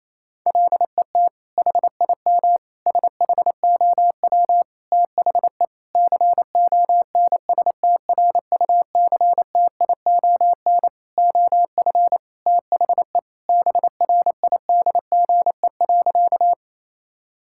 Morse code, heard as a keyed tone, about 27 wpm, 700 Hz. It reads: LET HIM SHOW THE CONSTRUCTION OF THE BRIDGE.
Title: Sentences from Top 1600 Words 28wpm